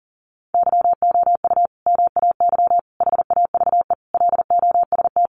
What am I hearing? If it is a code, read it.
Morse code, heard as a keyed tone, about 40 wpm, 700 Hz. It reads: YOU MAY HAVE LOST
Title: Sentences from Top 1400 Words 40wpm